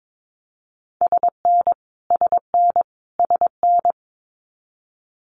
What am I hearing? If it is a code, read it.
Morse code, heard as a keyed tone, about 22 wpm, 700 Hz. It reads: SN SN SN
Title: Multi-letter Phonemes Repeated 3x 22wpm